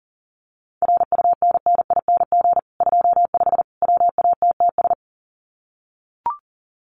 Morse code, heard as a keyed tone, about 37 wpm, 700 Hz. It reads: RUNNING 25 WATTS
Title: Common QSO Phrases 40wpm